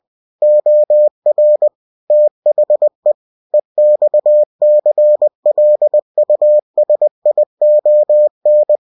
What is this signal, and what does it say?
Morse code, heard as a keyed tone, about 20 wpm, 600 Hz. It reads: OR THE EXCLUSION
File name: cw-books / winnie-the-pooh / winnie-20wpm-page-0119